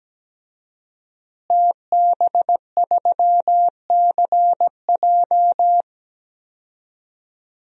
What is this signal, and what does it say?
Morse code, heard as a keyed tone, about 17 wpm, 700 Hz. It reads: TB3CJ
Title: DXCC-100 Call Signs - 5 Characters - Repeated 2x - 17wpm